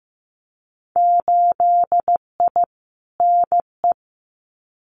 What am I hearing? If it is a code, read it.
Morse code, heard as a keyed tone, about 15 wpm, 700 Hz. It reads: 8I NE
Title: Winter Field Day Exchanges Encore 15wpm